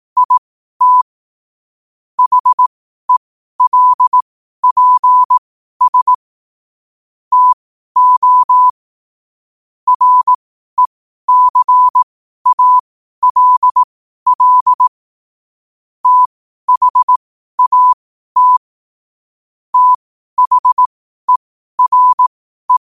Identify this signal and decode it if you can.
Morse code, heard as a keyed tone, about 18 wpm, spaced as for 12 wpm, 1000 Hz. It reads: IT HELPS TO RECALL THAT THERE
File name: qotdfiles / 12wpm